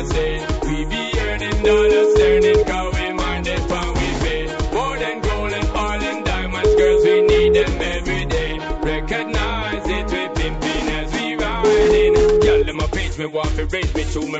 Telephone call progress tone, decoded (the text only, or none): ringback tone